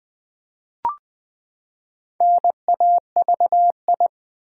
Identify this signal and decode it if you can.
Morse code, heard as a keyed tone, about 21 wpm, 700 Hz. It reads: NAVI
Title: Build A Word - 4 Letters At A Time - 10-Letter Words - 20wpm